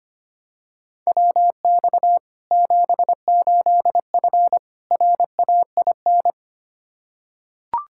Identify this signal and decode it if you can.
Morse code, heard as a keyed tone, about 25 wpm, 700 Hz. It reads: WX 78F RAIN
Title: K1USN SST QSO Practice 25wpm